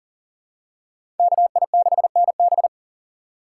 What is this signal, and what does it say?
Morse code, heard as a keyed tone, about 40 wpm, 700 Hz. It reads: KI6NB
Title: USA Call Signs - 3x Character Spacing - 40wpm